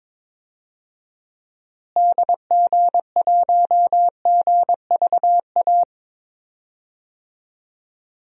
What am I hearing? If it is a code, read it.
Morse code, heard as a keyed tone, about 22 wpm, 700 Hz. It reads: DG1GVA
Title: DXCC-100 Call Signs - 6 Characters - Repeated 2x - 22wpm